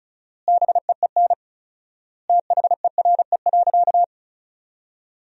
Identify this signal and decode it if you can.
Morse code, heard as a keyed tone, about 36 wpm, 700 Hz. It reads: BEEN THERE.
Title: 4x Word-Spacing - Sentences from Top 300 Words - 35wpm